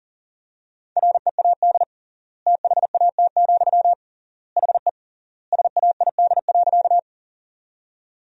Morse code, heard as a keyed tone, about 40 wpm, 700 Hz. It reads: READ THAT, HE SAID.
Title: 3x Word-Spacing - Sentences from Top 200 Words - 40wpm